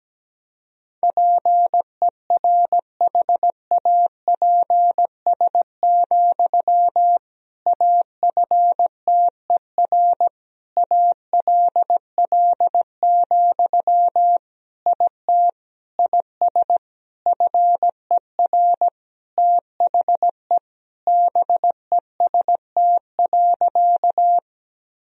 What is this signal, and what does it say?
Morse code, heard as a keyed tone, about 17 wpm, 700 Hz. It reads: PERHAPS, AFTER ALL, IT IS FER THE BEST.
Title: Sentences from Top 500 Words and Abbreviations 17wpm